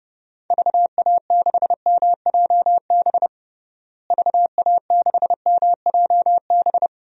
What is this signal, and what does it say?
Morse code, heard as a keyed tone, about 30 wpm, 700 Hz. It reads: VA6MJB VA6MJB
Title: DXCC-100 Call Signs - 6 Characters - Repeated 2x - 30wpm